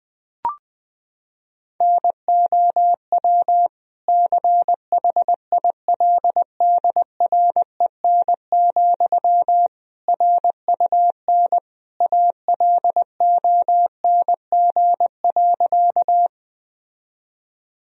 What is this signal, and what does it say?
Morse code, heard as a keyed tone, about 20 wpm, 700 Hz. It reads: NOW CHILDREN, RUN ALONG.
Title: Sentences from Top 1000 Words - Review Part 1 - 20wpm